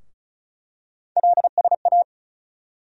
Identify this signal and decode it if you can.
Morse code, heard as a keyed tone, about 34 wpm, 700 Hz. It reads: LSA
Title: USA Call Sign Postfixes - 3 Letters 35wpm